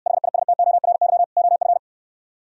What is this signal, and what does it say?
Morse code, heard as a keyed tone, about 68 wpm, 700 Hz. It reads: HISTORY OF